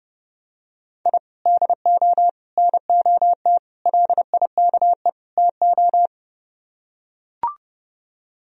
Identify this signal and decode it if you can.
Morse code, heard as a keyed tone, about 32 wpm, 700 Hz. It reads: I DO NOT LIKE TO